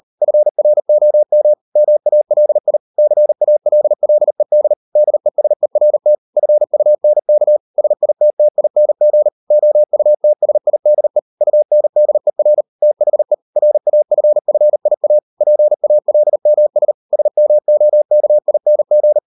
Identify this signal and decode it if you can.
Morse code, heard as a keyed tone, about 38 wpm, 600 Hz. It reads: FROM MALI CALLED DESERT FUNK SITTING OUTSIDE UNDER THE RAFFIA PALMS SMOKING